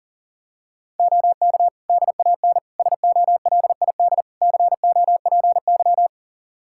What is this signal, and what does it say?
Morse code, heard as a keyed tone, about 40 wpm, 700 Hz. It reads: OK DAN SOLID COPY
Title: Practice QSO - Exchange 3 - More Info - 40wpm